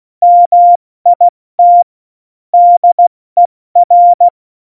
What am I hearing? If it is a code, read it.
Morse code, heard as a keyed tone, about 16 wpm, 700 Hz. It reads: MIT DER
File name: Morsekurs / Witze / Witz 04 August 2020, 70 F80